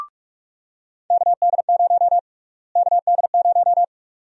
Morse code, heard as a keyed tone, about 42 wpm, 700 Hz. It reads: KD0 KD0
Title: USA Call Sign Prefixes - Repeated 2x - 45wpm